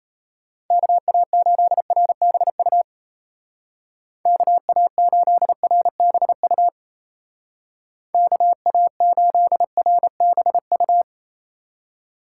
Morse code, heard as a keyed tone, about 35 wpm, 700 Hz. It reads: KA8RBU KA8RBU KA8RBU